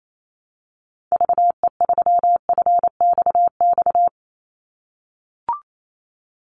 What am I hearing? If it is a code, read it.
Morse code, heard as a keyed tone, about 28 wpm, 700 Hz. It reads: VE3FXX
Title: Call Signs - 28wpm